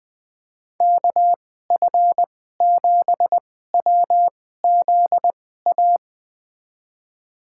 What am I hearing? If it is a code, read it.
Morse code, heard as a keyed tone, about 20 wpm, spaced as for 14 wpm, 700 Hz. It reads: KF7WZA
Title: USA Call Signs - 6 Characters - 2x Character Spacing - 20wpm